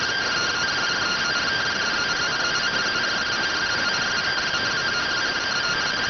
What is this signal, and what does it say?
Two-tone teleprinter signal, about 170 Hz shift.